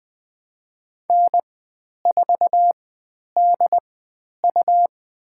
Morse code, heard as a keyed tone, about 20 wpm, spaced as for 10 wpm, 700 Hz. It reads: N4DU
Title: 109 Lesson 18g - Letter m - Call Signs 20wpm